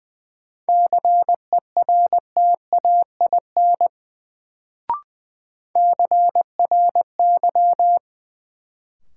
Morse code, heard as a keyed tone, about 20 wpm, 700 Hz. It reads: CERTAIN CRY
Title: Sentences from Top 500 Words - New Word Drill - 20wpm